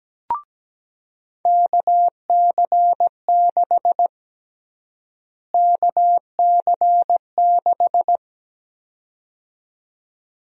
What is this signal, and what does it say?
Morse code, heard as a keyed tone, about 17 wpm, 700 Hz. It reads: KC6 KC6
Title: USA Call Sign Prefixes - Repeated 2x - 17wpm